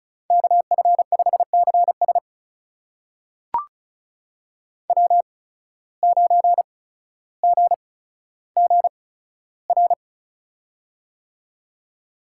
Morse code, heard as a keyed tone, about 36 wpm, spaced as for 15 wpm, 700 Hz. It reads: KF5CS W9GGR
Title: USA Call Signs - 5 Characters - 8x Character Spacing - 35wpm